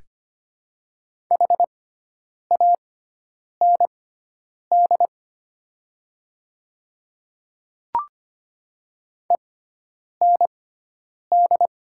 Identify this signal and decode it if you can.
Morse code, heard as a keyed tone, about 25 wpm, spaced as for 9 wpm, 700 Hz. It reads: HAND END
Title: 6x Character Spacing - Top 200 Words - 25wpm